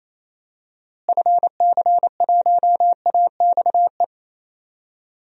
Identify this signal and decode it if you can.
Morse code, heard as a keyed tone, about 28 wpm, 700 Hz. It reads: FC1AXE
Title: DXCC-100 Call Signs - 6 Characters - Repeated 3x - 28wpm